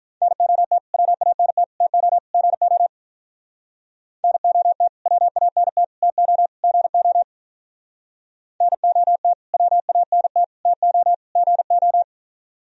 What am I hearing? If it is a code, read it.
Morse code, heard as a keyed tone, about 45 wpm, 700 Hz. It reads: NOT WANT TO GO NOT WANT TO GO NOT WANT TO GO